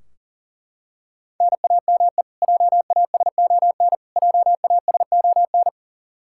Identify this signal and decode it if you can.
Morse code, heard as a keyed tone, about 40 wpm, 700 Hz. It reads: NAME JASON JASON
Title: Practice QSO - Exchange 1 - Beginning - 40wpm